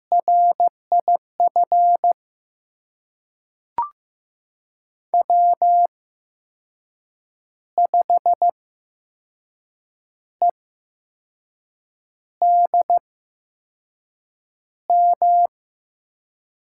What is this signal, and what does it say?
Morse code, heard as a keyed tone, about 15 wpm, spaced as for 5 wpm, 700 Hz. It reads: RIF W5EDM
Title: USA Call Signs - 5 Characters - 8x Character Spacing - 15wpm